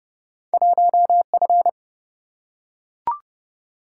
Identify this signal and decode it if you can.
Morse code, heard as a keyed tone, about 30 wpm, 700 Hz. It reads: 1F
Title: Single-Letter Single-Number 30wpm